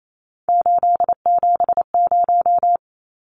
Morse code, heard as a keyed tone, about 28 wpm, 700 Hz. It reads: 870
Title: Numbers - 3-Digit 28wpm